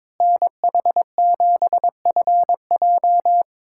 Morse code, heard as a keyed tone, about 22 wpm, 700 Hz. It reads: NH7FJ
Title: USA Callsign DE Callsign 22wpm